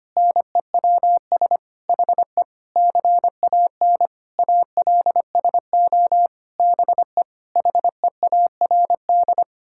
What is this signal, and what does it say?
Morse code, heard as a keyed tone, about 25 wpm, 700 Hz. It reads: NEWS HE CAN ALSO BE HEARD
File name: Sets of 9 Words - 25wpm